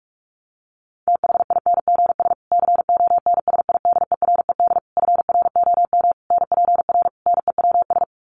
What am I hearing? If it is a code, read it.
Morse code, heard as a keyed tone, about 44 wpm, 700 Hz. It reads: THINGS CONSIDERED FROM NPR NEWS
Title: Sets of 5 Words Encore 45wpm